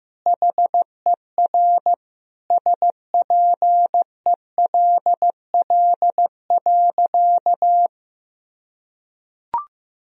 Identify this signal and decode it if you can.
Morse code, heard as a keyed tone, about 16 wpm, 700 Hz. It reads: HER SPELL.
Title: Sentences from Top 1000 Words - Review Part 1 - 15wpm